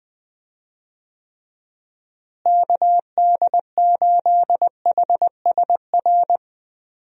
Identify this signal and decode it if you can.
Morse code, heard as a keyed tone, about 20 wpm, 700 Hz. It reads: KD8HSR
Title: USA Call Signs - 6 Characters - Repeated 2x - 20wpm